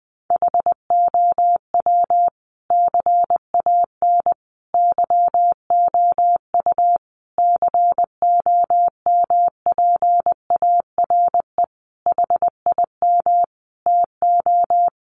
Morse code, heard as a keyed tone, about 20 wpm, 700 Hz. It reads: HOW CAN YOU COMPARE HIM TO
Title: Sentences from Top 1300 Words 20wpm